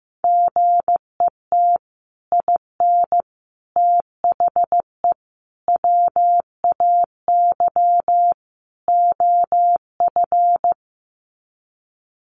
Morse code, heard as a keyed tone, about 15 wpm, 700 Hz. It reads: GET IN THE WAY OF
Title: Sets of 5 Words Encore 15wpm